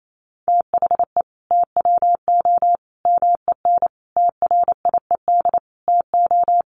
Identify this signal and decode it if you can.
Morse code, heard as a keyed tone, about 28 wpm, 700 Hz. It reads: THE TWO MEN TRIED TO
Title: A Sound Of Thunder 28wpm